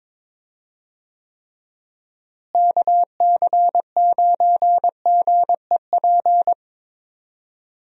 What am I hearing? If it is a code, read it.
Morse code, heard as a keyed tone, about 22 wpm, 700 Hz. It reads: KC9GEP
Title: USA Call Signs - 6 Characters - Repeated 2x - 22wpm